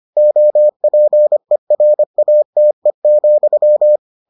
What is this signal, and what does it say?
Morse code, heard as a keyed tone, about 25 wpm, 600 Hz. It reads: OPERATE,